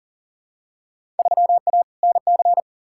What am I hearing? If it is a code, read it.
Morse code, heard as a keyed tone, about 40 wpm, 700 Hz. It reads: 3A NC
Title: Field Day Exchanges - 40wpm